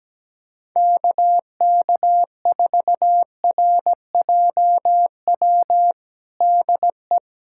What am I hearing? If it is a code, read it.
Morse code, heard as a keyed tone, about 17 wpm, 700 Hz. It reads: KK4RJW DE
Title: USA Callsign DE Callsign 17wpm